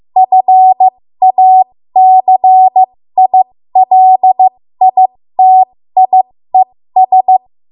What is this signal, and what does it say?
Morse code, heard as a keyed tone, about 15 wpm, spaced as for 13 wpm, 750 Hz. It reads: FACILITIES
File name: Morse / 13WPM